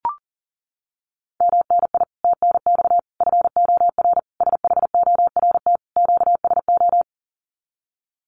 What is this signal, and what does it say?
Morse code, heard as a keyed tone, about 40 wpm, 700 Hz. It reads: MNI TNX FOR SHORT QSO
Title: Common QSO Phrases 40wpm